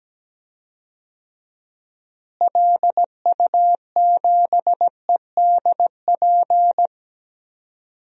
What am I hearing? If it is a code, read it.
Morse code, heard as a keyed tone, about 17 wpm, 700 Hz. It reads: LU7EDP
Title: DXCC-100 Call Signs - 6 Characters - Repeated 2x - 17wpm